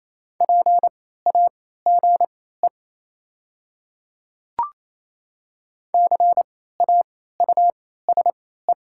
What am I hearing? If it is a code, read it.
Morse code, heard as a keyed tone, about 28 wpm, spaced as for 16 wpm, 700 Hz. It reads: PAGE CAUSE
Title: 3x Character Spacing - Top 500 Words - 28wpm